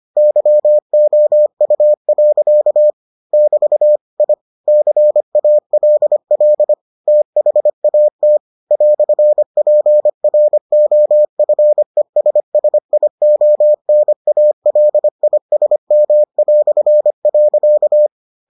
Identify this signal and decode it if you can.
Morse code, heard as a keyed tone, about 25 wpm, 600 Hz. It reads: YOU. <BT> I CALL THAT "PROFESSIONALISM".